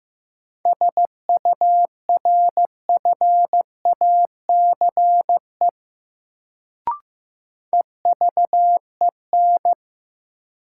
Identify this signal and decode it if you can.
Morse code, heard as a keyed tone, about 15 wpm, 700 Hz. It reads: SURFACE EVEN
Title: Sentences from Top 600 Words - New Word Drill - 15wpm